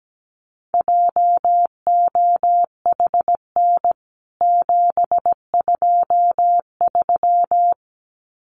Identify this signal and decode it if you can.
Morse code, heard as a keyed tone, about 17 wpm, 700 Hz. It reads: JOHN 723
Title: CWT Exchange 17wpm